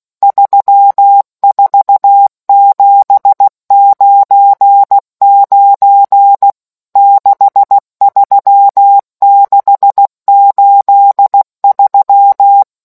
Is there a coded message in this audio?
Yes: Morse code from 0.2 to 12.6 s, heard as a keyed tone, about 16 wpm, 800 Hz. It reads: 34799 63683